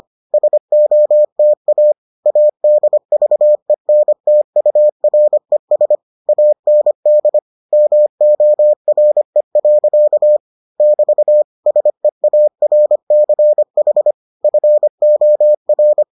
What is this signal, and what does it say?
Morse code, heard as a keyed tone, about 25 wpm, 600 Hz. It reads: SOTA ADVENTURES AND MORE. <BT> SEARCH FOR